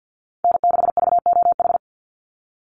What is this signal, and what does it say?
Morse code, heard as a keyed tone, about 49 wpm, 700 Hz. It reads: N6VOH
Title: Call Signs Repeated 3x - 50wpm